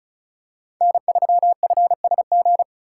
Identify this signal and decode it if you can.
Morse code, heard as a keyed tone, about 36 wpm, 700 Hz. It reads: N3FSG
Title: USA Call Signs - 5 Characters - 2x Character Spacing - 35wpm